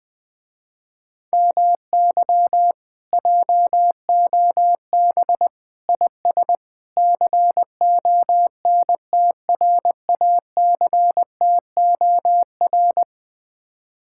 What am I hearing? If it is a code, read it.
Morse code, heard as a keyed tone, about 20 wpm, 700 Hz. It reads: MY JOB IS CONTRACTOR
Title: Speed-Racing Common QSO Phrases - 20wpm